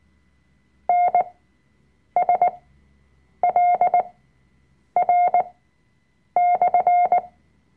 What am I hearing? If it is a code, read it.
Morse code, heard as a keyed tone, about 19 wpm, spaced as for 7 wpm, 710 Hz. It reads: NSLR/
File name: Workshop Dateien / CW-Online-Kurs / Lektion 005